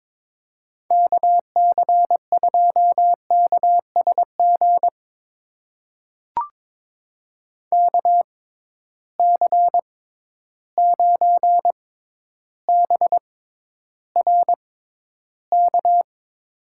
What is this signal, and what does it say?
Morse code, heard as a keyed tone, about 21 wpm, spaced as for 11 wpm, 700 Hz. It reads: KC2KSG KC9BRK
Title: USA Call Signs - 6 Characters - 6x Character Spacing - 22wpm